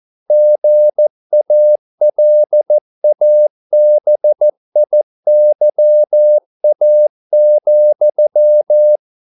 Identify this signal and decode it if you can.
Morse code, heard as a keyed tone, about 14 wpm, 600 Hz. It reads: GALABIYA,